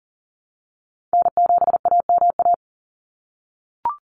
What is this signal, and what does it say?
Morse code, heard as a keyed tone, about 40 wpm, 700 Hz. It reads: N7AMA
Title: NA SOTA Chaser Callsigns 40wpm